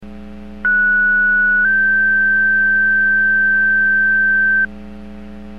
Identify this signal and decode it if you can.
Two-tone page, heard as a one-second tone, then a three-second tone, about 1470 then 1550 Hz. It is heard from 0.6 to 4.7 s.